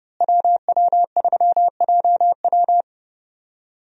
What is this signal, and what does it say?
Morse code, heard as a keyed tone, about 30 wpm, 700 Hz. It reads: WW3JW
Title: USA Call Signs - 5 Characters - Repeated 3x - 30wpm